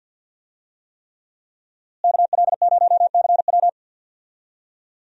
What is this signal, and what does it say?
Morse code, heard as a keyed tone, about 52 wpm, 690 Hz. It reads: KL0CW
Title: USA Call Signs - 5 Characters - Repeated 3x - 50wpm